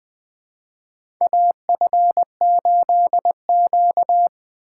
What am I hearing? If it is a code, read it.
Morse code, heard as a keyed tone, about 20 wpm, 700 Hz. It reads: AF8Q
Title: USA Call Signs - 4 Characters - 2x Character Spacing - 20wpm